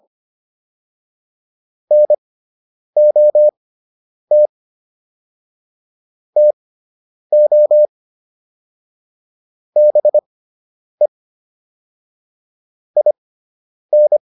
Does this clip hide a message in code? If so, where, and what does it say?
Yes: Morse code from 1.9 to 14.2 s, heard as a keyed tone, about 25 wpm, spaced as for 9 wpm, 600 Hz. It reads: NOT TO BE IN